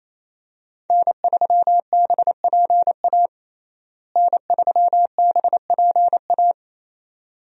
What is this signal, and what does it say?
Morse code, heard as a keyed tone, about 28 wpm, 700 Hz. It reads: N3BPA N3BPA
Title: USA Call Signs - 5 Characters - Repeated 2x - 28wpm